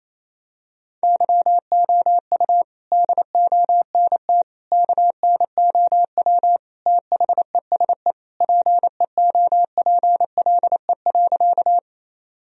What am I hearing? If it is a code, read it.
Morse code, heard as a keyed tone, about 28 wpm, 700 Hz. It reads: YOU DONT KNOW THESE PEOPLE.
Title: Sentences from Top 1000 Words - Review Part 1 - 28wpm